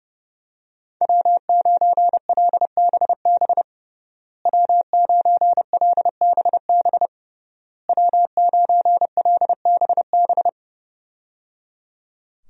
Morse code, heard as a keyed tone, about 30 wpm, 700 Hz. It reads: W9LBB W9LBB W9LBB
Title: USA Call Signs - 5 Characters - Repeated 3x - 30wpm